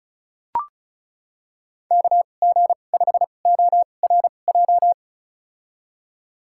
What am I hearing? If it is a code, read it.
Morse code, heard as a keyed tone, about 36 wpm, spaced as for 25 wpm, 700 Hz. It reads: KG5ORJ
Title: USA Call Signs - 6 Characters - 2x Character Spacing - 35wpm